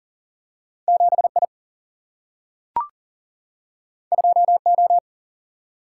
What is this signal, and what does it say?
Morse code, heard as a keyed tone, about 39 wpm, 700 Hz. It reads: 7I 2O
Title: Single-Letter Single-Number 40wpm